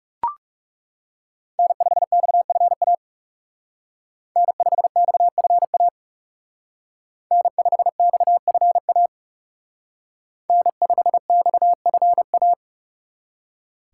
Morse code, heard as a keyed tone, about 39 wpm, 700 Hz. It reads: N5XFA N5XFA N5XFA N5XFA